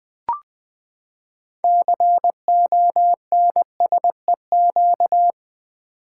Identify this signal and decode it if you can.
Morse code, heard as a keyed tone, about 20 wpm, 700 Hz. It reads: CONSEQ
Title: Build A Word - 6 Letters At A Time - 12-Letter Words - 20wpm